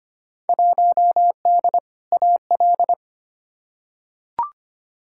Morse code, heard as a keyed tone, about 25 wpm, 700 Hz. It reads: 1D AL
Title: Field Day Exchanges - 25wpm